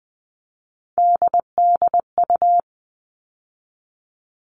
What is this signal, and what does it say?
Morse code, heard as a keyed tone, about 20 wpm, 700 Hz. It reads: DDU
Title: All Cut Numbers - 3-Digit 20wpm